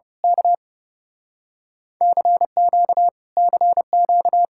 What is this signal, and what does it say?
Morse code, heard as a keyed tone, about 28 wpm, 700 Hz. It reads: K CQ CQ
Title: Speed-Racing Common QSO Phrases - 30wpm